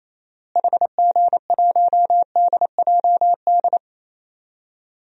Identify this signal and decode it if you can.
Morse code, heard as a keyed tone, about 28 wpm, 700 Hz. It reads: HG1DJD
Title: DXCC-100 Call Signs - 6 Characters - Repeated 3x - 28wpm